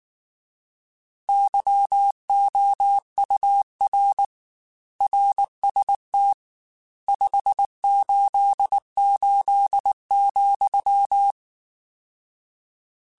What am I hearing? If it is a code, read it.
Morse code, heard as a keyed tone, about 19 wpm, 800 Hz. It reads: YOUR RST 588,
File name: RSTs / RST-mp3s / Your RSTs @18